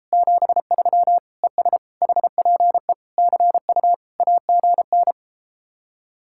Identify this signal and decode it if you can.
Morse code, heard as a keyed tone, about 33 wpm, 700 Hz. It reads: 73 ES HPE CU AGN